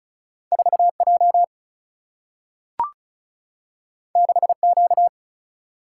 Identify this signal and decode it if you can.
Morse code, heard as a keyed tone, about 36 wpm, 700 Hz. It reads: 4J 6Q
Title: Single-Letter Single-Number 35wpm